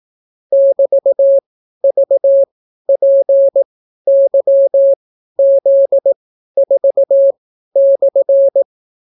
Morse code, heard as a keyed tone, about 18 wpm, spaced as for 12 wpm, 550 Hz. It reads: <BT>VPYZ4/